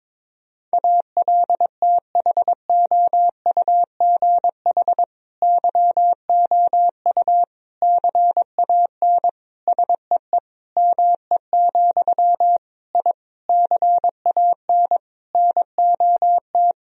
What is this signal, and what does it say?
Morse code, heard as a keyed tone, about 22 wpm, 700 Hz. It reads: ALTHOUGH YOU CAN SEE ME, I CAN NOT
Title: Sentences from Top 1000 Words - Review Part 3 - 22wpm